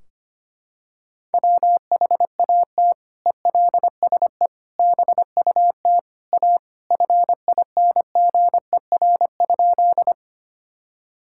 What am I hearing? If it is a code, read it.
Morse code, heard as a keyed tone, about 25 wpm, 700 Hz. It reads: WHAT ELSE BUT A FINGER?
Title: Sentences from Top 1600 Words 25wpm